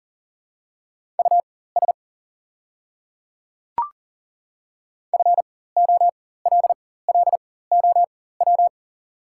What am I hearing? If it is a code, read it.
Morse code, heard as a keyed tone, about 38 wpm, spaced as for 18 wpm, 690 Hz. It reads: US FOLLOW
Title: 4x Character Spacing - Top 200 Words - 40wpm